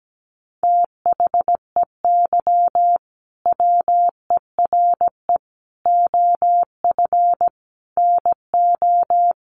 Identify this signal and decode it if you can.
Morse code, heard as a keyed tone, about 17 wpm, 700 Hz. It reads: THEY WERE OF NO
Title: Sentences from Top 1200 Words 17wpm